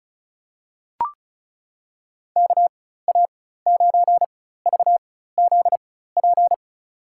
Morse code, heard as a keyed tone, about 36 wpm, spaced as for 16 wpm, 700 Hz. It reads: KA9VZP
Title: USA Call Signs - 4x Character Spacing - 35wpm